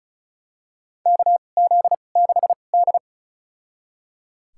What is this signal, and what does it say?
Morse code, heard as a keyed tone, about 36 wpm, spaced as for 26 wpm, 690 Hz. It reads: KZ6D
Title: USA Call Signs - 4 Characters - 2x Character Spacing - 35wpm